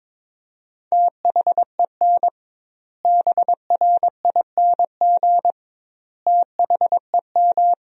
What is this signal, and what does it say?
Morse code, heard as a keyed tone, about 22 wpm, 700 Hz. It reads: THEN BRING THEM
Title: 2x Word-Spacing - Sentences from Top 500 Words - 22wpm